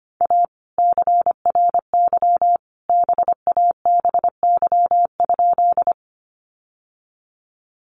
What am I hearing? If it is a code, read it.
Morse code, heard as a keyed tone, about 25 wpm, 700 Hz. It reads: A CRY BABY?
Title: Sentences from Top 1300 Words 25wpm